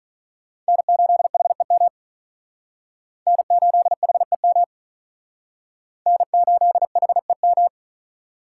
Morse code, heard as a keyed tone, about 42 wpm, 700 Hz. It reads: N8HEM N8HEM N8HEM
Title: Speed Racing - USA Call Signs 35wpm